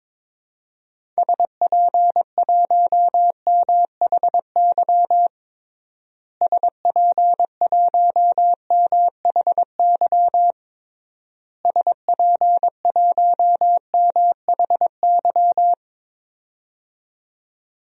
Morse code, heard as a keyed tone, about 21 wpm, 700 Hz. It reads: SP1MHY SP1MHY SP1MHY
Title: DXCC-100 Call Signs - 6 Characters - Repeated 3x - 22wpm